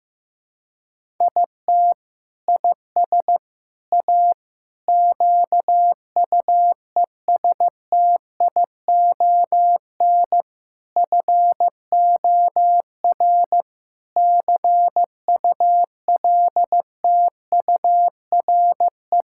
Morse code, heard as a keyed tone, about 15 wpm, 700 Hz. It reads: IT IS A QUESTION FOR CULTURE
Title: Sentences from Top 1000 Words - Review Part 6 - 15wpm